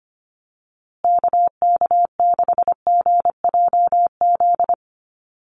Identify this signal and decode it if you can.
Morse code, heard as a keyed tone, about 25 wpm, 700 Hz. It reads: KK6GJZ
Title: USA Call Signs Repeated 3x 25wpm